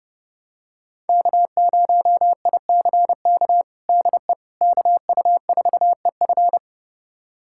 Morse code, heard as a keyed tone, about 32 wpm, 700 Hz. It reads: K0ICK DE KU4EF
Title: USA Callsign DE Callsign - 5 Characters - 30wpm